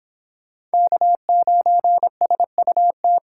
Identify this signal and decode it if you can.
Morse code, heard as a keyed tone, about 30 wpm, 700 Hz. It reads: K9SUT